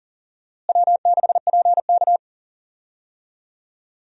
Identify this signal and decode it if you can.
Morse code, heard as a keyed tone, about 40 wpm, 700 Hz. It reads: W6PK
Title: USA Call Signs - 4 Characters - Repeated 2x - 40wpm